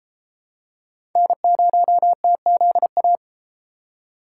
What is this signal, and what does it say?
Morse code, heard as a keyed tone, about 33 wpm, 700 Hz. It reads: N0TZA